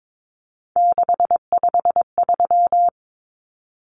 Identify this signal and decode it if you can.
Morse code, heard as a keyed tone, about 22 wpm, 700 Hz. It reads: 653
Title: Common Cut Numbers - 3-Digit 22wpm